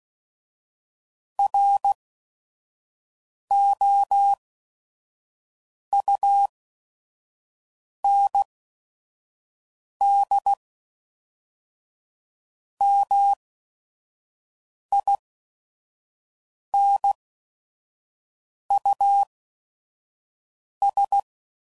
Morse code, heard as a keyed tone, about 16 wpm, spaced as for 5 wpm, 800 Hz. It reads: ROUND MINUS
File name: CW Code Groups @5WPM / QSOs / QSO WD0FBT